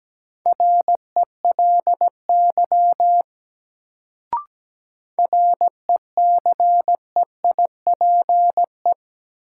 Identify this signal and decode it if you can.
Morse code, heard as a keyed tone, about 17 wpm, 700 Hz. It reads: RELY RECEIPE
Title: Sentences from Top 1200 Words - New Word Drill Alt - 17wpm